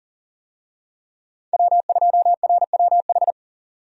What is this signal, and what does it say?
Morse code, heard as a keyed tone, about 40 wpm, 700 Hz. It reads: W2RWH
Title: USA Call Signs - 5 Characters - 40wpm